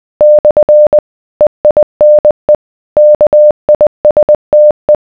Morse code, heard as a keyed tone, about 20 wpm, 600 Hz. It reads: / EINE KISTE